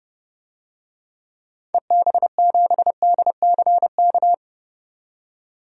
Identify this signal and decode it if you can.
Morse code, heard as a keyed tone, about 32 wpm, 700 Hz. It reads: EB7DCK
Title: DXCC-100 Call Signs - 6 Characters - 30wpm